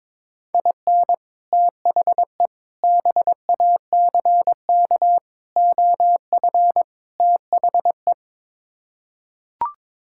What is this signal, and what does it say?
Morse code, heard as a keyed tone, about 22 wpm, 700 Hz. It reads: IN THE BACK OF THE
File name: practice / 22wpm / Build A Phrase - 5-Word Phrases - 22wpm